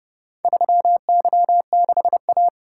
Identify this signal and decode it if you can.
Morse code, heard as a keyed tone, about 29 wpm, 700 Hz. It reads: 3Y6A